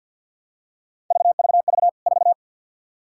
Morse code, heard as a keyed tone, about 49 wpm, 700 Hz. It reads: VVV 4
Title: K1USN SST QSO Practice 50wpm